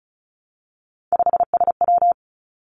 Morse code, heard as a keyed tone, about 36 wpm, 700 Hz. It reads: 5SW
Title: Three Challenging Letter-Number Combinations 35wpm